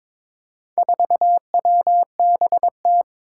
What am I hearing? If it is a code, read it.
Morse code, heard as a keyed tone, about 22 wpm, 700 Hz. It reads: 4WBT
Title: Four Challenging Letter-Number Combinations 22wpm